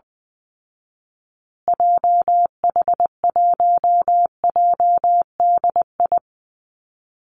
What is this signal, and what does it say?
Morse code, heard as a keyed tone, about 20 wpm, 700 Hz. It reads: JH1JDI
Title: Call Signs Repeated 2x - 20wpm